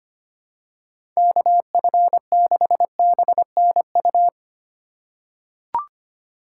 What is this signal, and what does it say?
Morse code, heard as a keyed tone, about 25 wpm, 700 Hz. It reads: KF6BNU
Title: USA Call Signs - 3x Character Spacing - 25wpm